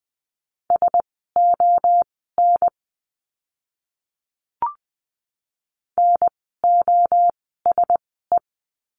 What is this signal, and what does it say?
Morse code, heard as a keyed tone, about 20 wpm, spaced as for 14 wpm, 700 Hz. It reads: SON NOSE
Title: 054 Lesson 10f - Letter r - Words 20wpm